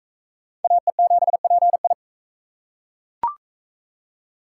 Morse code, heard as a keyed tone, about 40 wpm, 700 Hz. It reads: AE7PI